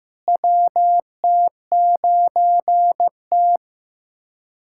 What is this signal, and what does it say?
Morse code, heard as a keyed tone, about 16 wpm, 700 Hz. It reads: WT9T
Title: USA Callsign DE Callsign - 4 Characters - 15wpm